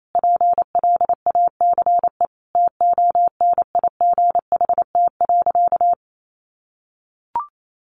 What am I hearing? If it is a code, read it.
Morse code, heard as a keyed tone, about 28 wpm, 700 Hz. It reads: PLACE TONIGHT.
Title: Sentences from Top 1300 Words 28wpm